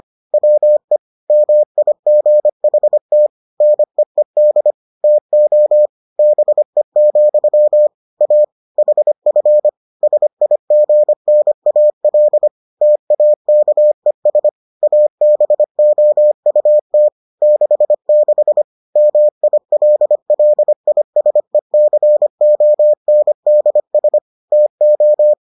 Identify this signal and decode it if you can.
Morse code, heard as a keyed tone, about 25 wpm, 600 Hz. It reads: WE MIGHT NEED TO BE, A HF SIGNAL TAKES ABOUT 66 MILLISECONDS TO